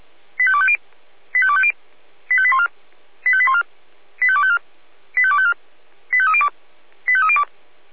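Five-tone selective call (ZVEI), heard as quick runs of five tones, tones of about 70 ms. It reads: 85250 85250 85814 85814 85254 85254 85302 85302
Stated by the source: ZVEI five-tone selective call 85310 received by SDR at Feuerwehr Orlamünde.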